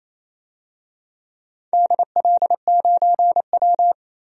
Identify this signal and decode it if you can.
Morse code, heard as a keyed tone, about 28 wpm, 700 Hz. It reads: DL9W
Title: DXCC-100 Call Signs - 4 Characters - 28wpm